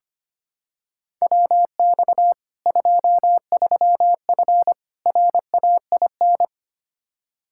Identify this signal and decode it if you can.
Morse code, heard as a keyed tone, about 25 wpm, 700 Hz. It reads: WX 23F RAIN
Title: Practice QSO - Exchange 2 - Info - 25wpm